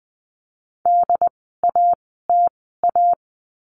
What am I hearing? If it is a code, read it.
Morse code, heard as a keyed tone, about 20 wpm, spaced as for 14 wpm, 700 Hz. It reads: DATA
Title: 075 Lesson 13f - Letter L - Words 20wpm